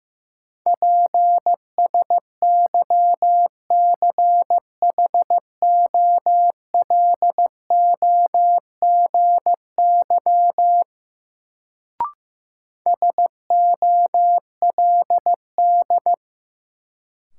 Morse code, heard as a keyed tone, about 15 wpm, 700 Hz. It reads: PSYCHOLOGY SOLD
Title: Sentences from Top 1400 Words - New Word Drill Alt - 15wpm